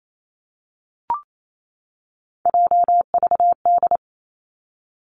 Morse code, heard as a keyed tone, about 28 wpm, 700 Hz. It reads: JVD
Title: Three Challenging Letter-Number Combinations 28wpm